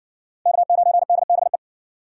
Morse code, heard as a keyed tone, about 56 wpm, 700 Hz. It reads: K9DBE